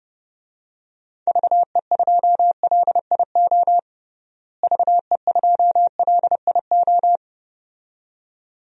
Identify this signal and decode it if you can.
Morse code, heard as a keyed tone, about 32 wpm, 700 Hz. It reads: VE2LIO VE2LIO
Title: DXCC-100 Call Signs - 6 Characters - Repeated 2x - 30wpm